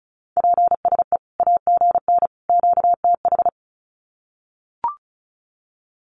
Morse code, heard as a keyed tone, about 34 wpm, 700 Hz. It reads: PSE AGN QTH
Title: Common QSO Phrases 35wpm